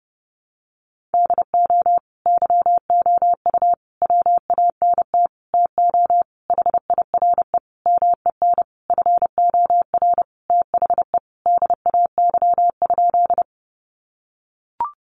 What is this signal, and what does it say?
Morse code, heard as a keyed tone, about 30 wpm, 700 Hz. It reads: DO YOU WANT TO HIRE MEN FOR THE DAY?
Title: Sentences from Top 1400 Words 30wpm